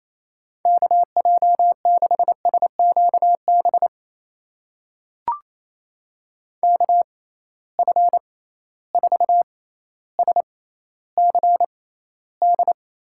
Morse code, heard as a keyed tone, about 28 wpm, spaced as for 14 wpm, 700 Hz. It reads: KJ6SQB KF4SCD
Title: USA Call Signs - 6 Characters - 6x Character Spacing - 28wpm